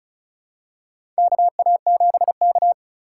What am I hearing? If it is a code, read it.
Morse code, heard as a keyed tone, about 36 wpm, 700 Hz. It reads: KA7K
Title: USA Call Signs - 4 Characters - 2x Character Spacing - 35wpm